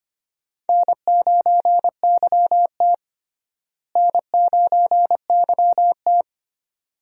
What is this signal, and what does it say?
Morse code, heard as a keyed tone, about 25 wpm, 700 Hz. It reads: N9YT N9YT
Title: USA Call Signs - 4 Characters - Repeated 2x - 25wpm